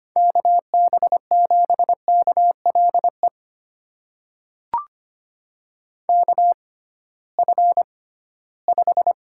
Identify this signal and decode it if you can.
Morse code, heard as a keyed tone, about 25 wpm, spaced as for 16 wpm, 700 Hz. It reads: KB7KLE KF5
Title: USA Call Signs - 6 Characters - 6x Character Spacing - 25wpm